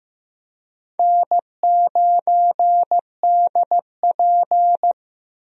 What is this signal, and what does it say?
Morse code, heard as a keyed tone, about 15 wpm, 700 Hz. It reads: N9DP
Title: USA Call Signs - 4 Characters - Repeated 3x - 15wpm